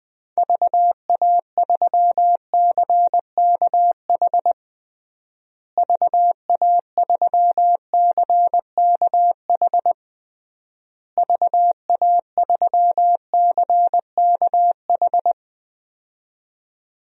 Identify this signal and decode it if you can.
Morse code, heard as a keyed tone, about 20 wpm, 700 Hz. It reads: VA3CKH VA3CKH VA3CKH
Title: DXCC-100 Call Signs - 6 Characters - Repeated 3x - 20wpm